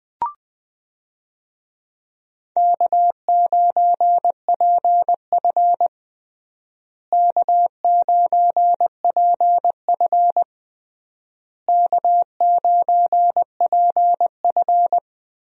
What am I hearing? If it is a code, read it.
Morse code, heard as a keyed tone, about 20 wpm, 700 Hz. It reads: K9PF K9PF K9PF
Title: USA Call Signs - 4 Characters - Repeated 3x - 20wpm